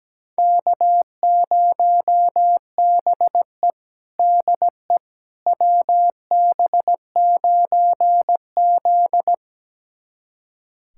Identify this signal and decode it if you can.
Morse code, heard as a keyed tone, about 17 wpm, 700 Hz. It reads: K0BE DE WB9Z
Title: USA Callsign DE Callsign - 4 Characters - 17wpm